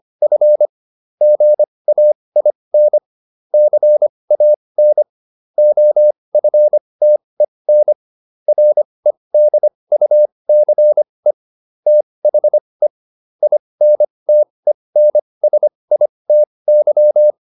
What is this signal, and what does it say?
Morse code, heard as a keyed tone, about 25 wpm, spaced as for 20 wpm, 600 Hz. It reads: F GAIN CAN OFTEN REDUCE THE INTENSITY